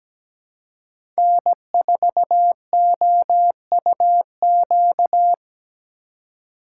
Morse code, heard as a keyed tone, about 17 wpm, 700 Hz. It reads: N4OUQ
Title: USA Call Signs - 3x Character Spacing - 17wpm